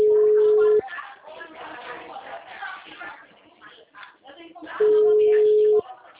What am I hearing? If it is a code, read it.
Ringback tone (Europe, 425 Hz), with speech.